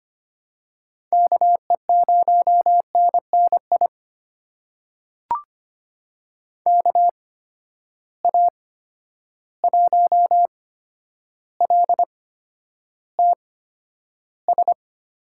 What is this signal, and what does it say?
Morse code, heard as a keyed tone, about 25 wpm, spaced as for 11 wpm, 700 Hz. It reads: KE0NNI KA1LTS
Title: USA Call Signs - 6 Characters - 8x Character Spacing - 25wpm